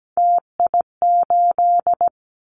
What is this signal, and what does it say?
Morse code, heard as a keyed tone, about 17 wpm, 700 Hz. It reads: TI8
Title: Callsign Prefixes for Top 100 DXCC - 17wpm